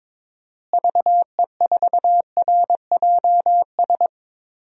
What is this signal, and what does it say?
Morse code, heard as a keyed tone, about 22 wpm, 700 Hz. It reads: VE4RJS
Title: DXCC-100 Call Signs - 6 Characters - Repeated 3x - 22wpm